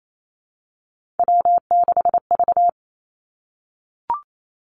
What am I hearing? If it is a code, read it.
Morse code, heard as a keyed tone, about 28 wpm, 700 Hz. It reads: W6V
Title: Three Challenging Letter-Number Combinations 28wpm